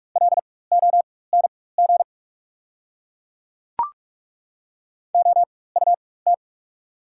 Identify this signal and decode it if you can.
Morse code, heard as a keyed tone, about 43 wpm, spaced as for 21 wpm, 700 Hz. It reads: LONG OUT
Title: 4x Character Spacing - Top 300 Words - 45wpm